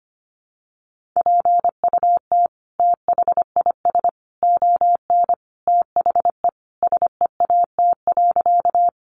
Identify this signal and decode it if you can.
Morse code, heard as a keyed tone, about 25 wpm, 700 Hz. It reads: PUT THIS ON THE SEAT.
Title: Sentences from Top 1500 Words 25wpm